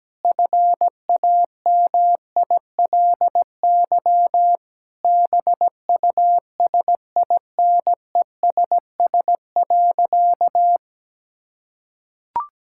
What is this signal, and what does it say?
Morse code, heard as a keyed tone, about 17 wpm, 700 Hz. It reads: FAMILY BUSINESS.
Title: Sentences from Top 1000 Words - Review Part 6 - 17wpm